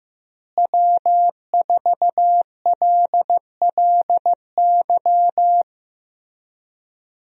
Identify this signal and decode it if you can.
Morse code, heard as a keyed tone, about 15 wpm, 700 Hz. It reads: W4LLY
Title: USA Call Signs - 5 Characters - Repeated 2x - 15wpm